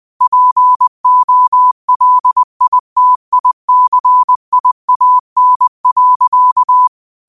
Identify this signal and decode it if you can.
Morse code, heard as a keyed tone, about 20 wpm, 1000 Hz. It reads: POLITICIAN.
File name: qotdfiles / 20wpm